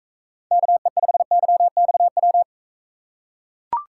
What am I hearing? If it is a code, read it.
Morse code, heard as a keyed tone, about 42 wpm, 700 Hz. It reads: KE5YXW